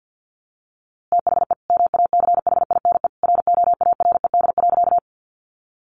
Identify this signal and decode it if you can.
Morse code, heard as a keyed tone, about 50 wpm, 700 Hz. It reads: THE MACHINE ROARED.
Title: A Sound Of Thunder 50wpm